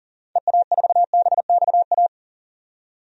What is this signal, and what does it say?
Morse code, heard as a keyed tone, about 37 wpm, 700 Hz. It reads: EA4BXA
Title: DXCC-100 Call Signs - 6 Characters - 40wpm